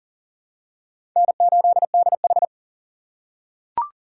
Morse code, heard as a keyed tone, about 40 wpm, 700 Hz. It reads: N8DH
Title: USA Call Signs - 3x Character Spacing - 40wpm